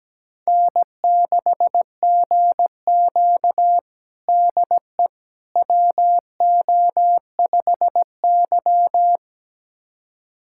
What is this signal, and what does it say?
Morse code, heard as a keyed tone, about 17 wpm, 700 Hz. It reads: N6GQ DE WO5Y
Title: USA Callsign DE Callsign - 4 Characters - 17wpm